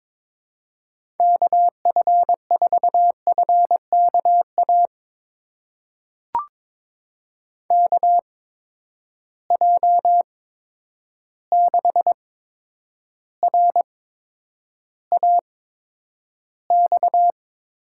Morse code, heard as a keyed tone, about 22 wpm, spaced as for 9 wpm, 700 Hz. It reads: KF4FKA KJ6RAX
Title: USA Call Signs - 6 Characters - 8x Character Spacing - 22wpm